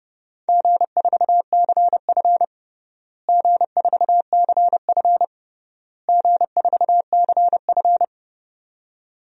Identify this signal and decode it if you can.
Morse code, heard as a keyed tone, about 31 wpm, 700 Hz. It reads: G4CF G4CF G4CF
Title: DXCC-100 Call Signs - 4 Characters - Repeated 3x - 30wpm